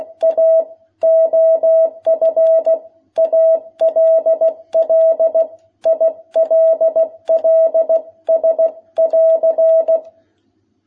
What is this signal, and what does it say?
Morse code, heard as a keyed tone, about 16 wpm, 660 Hz. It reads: A OF ALL ILLS<AR>
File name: jh8cbh / 2021cw